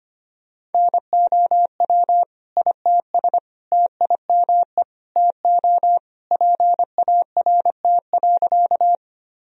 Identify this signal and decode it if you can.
Morse code, heard as a keyed tone, about 25 wpm, 700 Hz. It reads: NOW ITS TIME TO PART.
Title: Sentences from Top 1000 Words - Review Part 1 - 25wpm